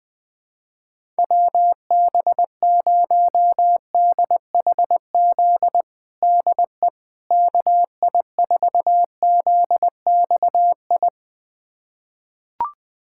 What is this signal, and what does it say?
Morse code, heard as a keyed tone, about 20 wpm, 700 Hz. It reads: WB0DHZ DE KI4ZXI
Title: USA Callsign DE Callsign - 6 Characters - 20wpm